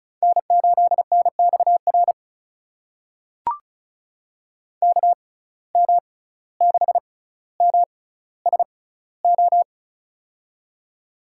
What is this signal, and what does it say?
Morse code, heard as a keyed tone, about 34 wpm, spaced as for 17 wpm, 700 Hz. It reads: N8NXR KM6MSO
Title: USA Call Signs - 6x Character Spacing - 35wpm